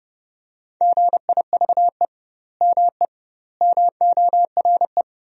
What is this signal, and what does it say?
Morse code, heard as a keyed tone, about 30 wpm, 700 Hz. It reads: GIVE ME MORE
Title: 2x Word-Spacing - Sentences from Top 200 Words - 30wpm